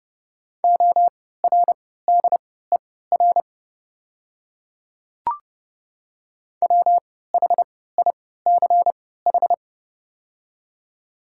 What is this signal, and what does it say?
Morse code, heard as a keyed tone, about 30 wpm, spaced as for 17 wpm, 700 Hz. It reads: ORDER WHICH
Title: 3x Character Spacing - Top 500 Words - 30wpm